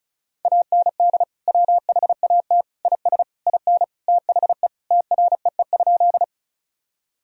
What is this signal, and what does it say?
Morse code, heard as a keyed tone, about 36 wpm, 700 Hz. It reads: AND WHAT IS IN THE TREE?
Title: Sentences from Top 1000 Words - Review Part 1 - 35wpm